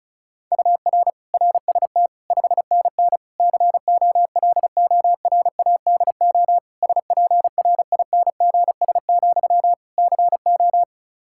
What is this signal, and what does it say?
Morse code, heard as a keyed tone, about 36 wpm, 700 Hz. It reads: UR RST 5NN COLORADO SPRINGS, CO
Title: Practice QSO - Exchange 1 - Beginning - 35wpm